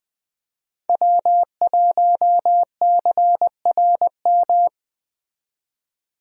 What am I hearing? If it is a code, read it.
Morse code, heard as a keyed tone, about 20 wpm, 700 Hz. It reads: W1CRM
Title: USA Call Signs - 5 Characters - 20wpm